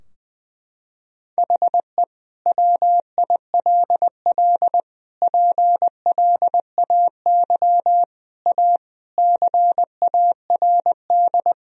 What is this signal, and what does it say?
Morse code, heard as a keyed tone, about 21 wpm, 700 Hz. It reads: HE WILL PLAY A CARD
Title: Sentences from Top 1000 Words - Review Part 4 - 20wpm